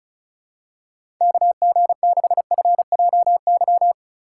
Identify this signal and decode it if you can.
Morse code, heard as a keyed tone, about 36 wpm, 700 Hz. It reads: KG6FJY
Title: USA Call Signs - 6 Characters - 6x Character Spacing - 35wpm